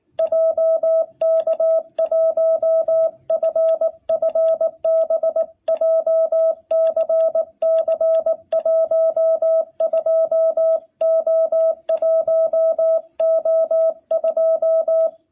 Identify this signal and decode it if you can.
Morse code, heard as a keyed tone, about 18 wpm, 660 Hz. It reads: JK1FFB JCC12O1O2
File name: jh8cbh / 2021cw